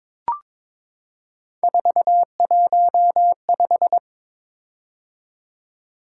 Morse code, heard as a keyed tone, about 22 wpm, 700 Hz. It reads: 415
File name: practice / 22wpm / Three Letter-Number Combinations - 22wpm